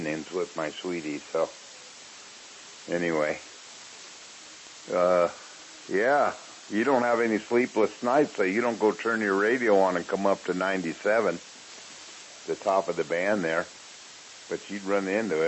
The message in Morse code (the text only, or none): none